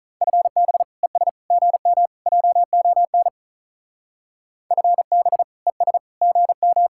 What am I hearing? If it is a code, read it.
Morse code, heard as a keyed tone, about 45 wpm, 700 Hz. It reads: FB ES GM JON FB ES GM